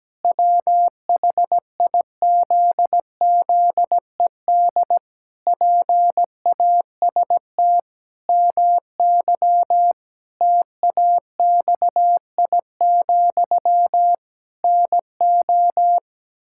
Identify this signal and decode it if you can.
Morse code, heard as a keyed tone, about 17 wpm, 700 Hz. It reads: WHIZZED PAST MY TAXI, NO
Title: Pangram Challenge v2 - 17wpm